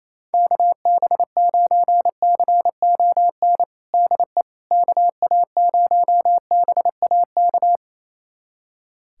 Morse code, heard as a keyed tone, about 28 wpm, 700 Hz. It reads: KB9CON DE KA0BAK
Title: USA Callsign DE Callsign 28wpm